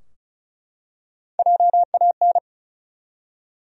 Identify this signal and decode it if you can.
Morse code, heard as a keyed tone, about 36 wpm, 700 Hz. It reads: JAN
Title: USA Common Names - 35wpm